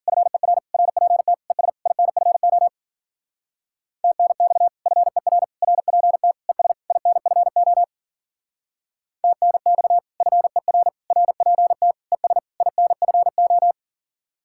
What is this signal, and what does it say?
Morse code, heard as a keyed tone, about 52 wpm, 700 Hz. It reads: FER RPT ES INFO TNX FER RPT ES INFO TNX FER RPT ES INFO